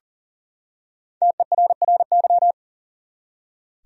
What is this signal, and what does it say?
Morse code, heard as a keyed tone, about 40 wpm, 700 Hz. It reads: TERRY
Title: USA Common Names - Max 5 Letters - 40wpm